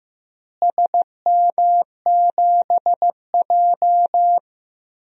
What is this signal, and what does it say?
Morse code, heard as a keyed tone, about 15 wpm, 700 Hz. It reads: SM7J
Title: DXCC-100 Call Signs - 4 Characters - Repeated 2x - 15wpm